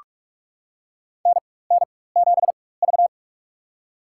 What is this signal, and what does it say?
Morse code, heard as a keyed tone, about 46 wpm, spaced as for 21 wpm, 700 Hz. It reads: NN7V
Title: USA Call Signs - 4 Characters - 4x Character Spacing - 45wpm